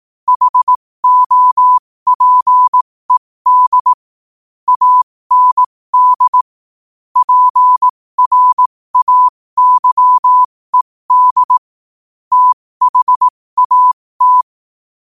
Morse code, heard as a keyed tone, about 18 wpm, spaced as for 15 wpm, 1000 Hz. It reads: HOPED AND PRAYED THAT
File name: qotdfiles / 15wpm